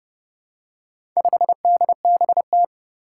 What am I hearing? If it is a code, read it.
Morse code, heard as a keyed tone, about 31 wpm, 700 Hz. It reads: 5DBT
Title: Four Challenging Letter-Number Combinations 30wpm